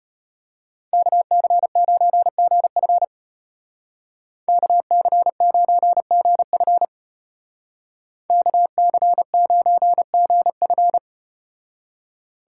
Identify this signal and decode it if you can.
Morse code, heard as a keyed tone, about 35 wpm, 700 Hz. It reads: KC9GF KC9GF KC9GF